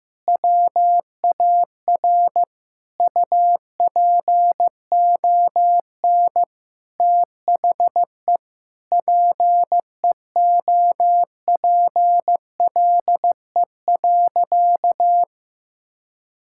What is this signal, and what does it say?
Morse code, heard as a keyed tone, about 15 wpm, 700 Hz. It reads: WAR UPON THE PEOPLE.
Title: Sentences from Top 1000 Words - Review Part 6 - 15wpm